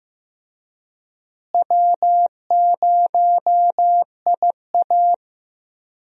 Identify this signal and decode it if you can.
Morse code, heard as a keyed tone, about 16 wpm, 700 Hz. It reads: W0IA
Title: USA Call Signs - 4 Characters - Repeated 3x - 15wpm